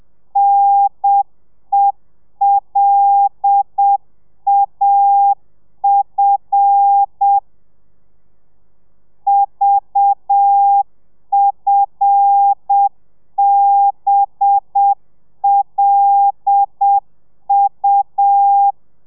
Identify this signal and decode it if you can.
Morse code, heard as a keyed tone, about 7 wpm, 800 Hz. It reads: NELAF VFBLU
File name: telegrafia / i2tfj / Lez5 30